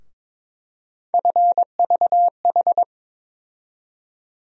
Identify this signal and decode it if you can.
Morse code, heard as a keyed tone, about 21 wpm, 700 Hz. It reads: FVH
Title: USA Call Sign Postfixes - 3 Letters 22wpm